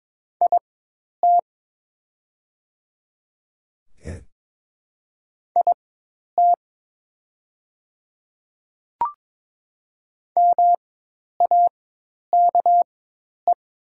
Morse code, heard as a keyed tone, about 22 wpm, spaced as for 10 wpm, 700 Hz. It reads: IT IT MAKE
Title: 4x Character Spacing - Top 100 Words - 22wpm